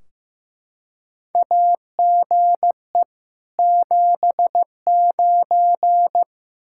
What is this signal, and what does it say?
Morse code, heard as a keyed tone, about 15 wpm, 700 Hz. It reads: AGE 79
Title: Practice QSO - Exchange 3 - More Info - 15wpm